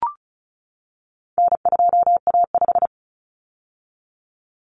Morse code, heard as a keyed tone, about 34 wpm, 700 Hz. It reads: N2A5
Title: Common Cut Numbers - 4-Digit 35wpm